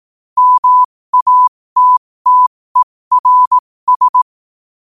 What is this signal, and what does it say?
Morse code, heard as a keyed tone, about 18 wpm, spaced as for 15 wpm, 1000 Hz. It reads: MATTERS
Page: 15 WPM morse code quotes for Tue, 23 Sep 2025.